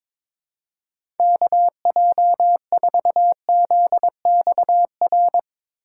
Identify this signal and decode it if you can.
Morse code, heard as a keyed tone, about 22 wpm, 700 Hz. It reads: KJ4ZXR
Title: USA Call Signs - 6 Characters - 2x Character Spacing - 22wpm